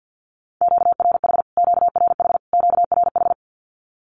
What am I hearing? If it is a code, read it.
Morse code, heard as a keyed tone, about 48 wpm, 700 Hz. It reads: QRH QRH QRH
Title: Q-Codes Repeated 3x 50wpm